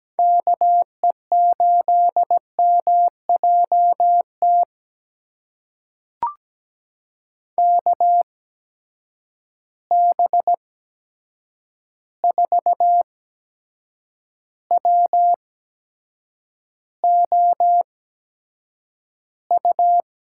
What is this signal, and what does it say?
Morse code, heard as a keyed tone, about 17 wpm, spaced as for 7 wpm, 700 Hz. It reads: KE8MJT KB4WOU
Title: USA Call Signs - 6 Characters - 8x Character Spacing - 17wpm